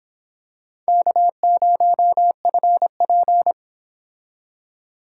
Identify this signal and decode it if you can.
Morse code, heard as a keyed tone, about 26 wpm, 700 Hz. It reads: K0FP